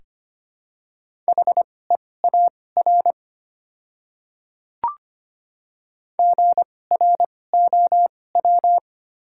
Morse code, heard as a keyed tone, about 25 wpm, spaced as for 18 wpm, 700 Hz. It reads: HEAR GROW
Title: 2x Character Spacing - Top 300 Words - 25wpm